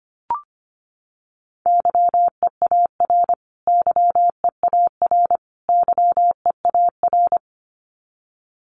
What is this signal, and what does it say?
Morse code, heard as a keyed tone, about 25 wpm, 700 Hz. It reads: YEAR YEAR YEAR
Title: Top 200 Words Repeated 25wpm